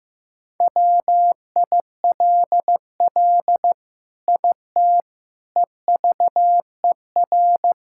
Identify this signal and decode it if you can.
Morse code, heard as a keyed tone, about 16 wpm, 700 Hz. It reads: WILL IT EVER
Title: Sentences from Top 1000 Words - Review Part 5 - 15wpm